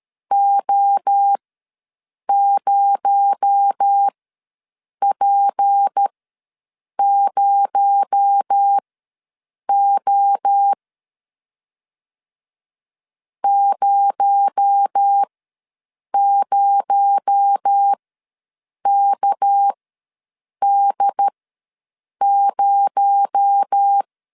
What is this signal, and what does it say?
Morse code, heard as a keyed tone, about 13 wpm, spaced as for 7 wpm, 800 Hz. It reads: O0P0O 00KD0
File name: LEZIONI / 6 PKX0 / 6 PKX0 ws 12wpm 3448